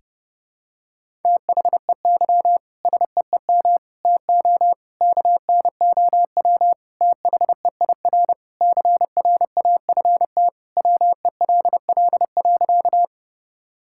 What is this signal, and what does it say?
Morse code, heard as a keyed tone, about 30 wpm, 700 Hz. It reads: THEY SEEM TO KNOW THEIR CRAFT WELL.
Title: Sentences from Top 1000 Words - Review Part 6 - 30wpm